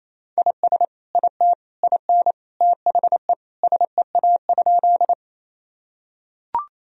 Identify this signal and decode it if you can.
Morse code, heard as a keyed tone, about 28 wpm, 700 Hz. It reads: IS IT IN THE SEA?
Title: Sentences from Top 1000 Words - Review Part 1 - 28wpm